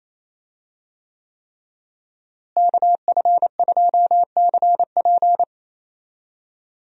Morse code, heard as a keyed tone, about 27 wpm, 700 Hz. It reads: KF2CP
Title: USA Call Signs - 5 Characters - Repeated 3x - 28wpm